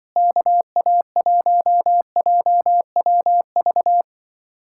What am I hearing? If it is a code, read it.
Morse code, heard as a keyed tone, about 25 wpm, 700 Hz. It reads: KA1JWV
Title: Speed-Racing Common QSO Phrases - 20wpm